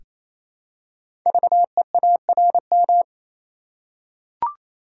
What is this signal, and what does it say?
Morse code, heard as a keyed tone, about 27 wpm, 700 Hz. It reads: VEARM
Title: Code Groups - 5 Letters - 28wpm